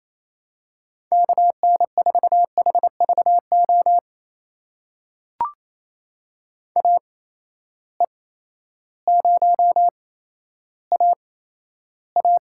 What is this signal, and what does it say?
Morse code, heard as a keyed tone, about 28 wpm, spaced as for 13 wpm, 700 Hz. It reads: KN4HVO AE0AA
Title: USA Call Signs - 8x Character Spacing - 28wpm